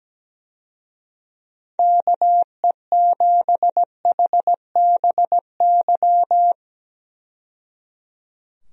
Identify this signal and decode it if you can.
Morse code, heard as a keyed tone, about 17 wpm, 700 Hz. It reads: KE7HBY
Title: USA Call Signs - 6 Characters - 17wpm